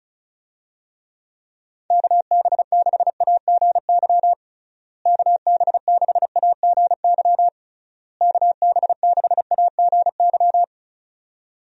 Morse code, heard as a keyed tone, about 36 wpm, 700 Hz. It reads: KB6AGY KB6AGY KB6AGY
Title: USA Call Signs - 6 Characters - Repeated 3x - 35wpm